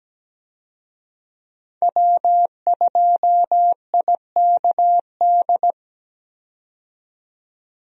Morse code, heard as a keyed tone, about 17 wpm, 700 Hz. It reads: W2IKD
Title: USA Call Signs - 5 Characters - 17wpm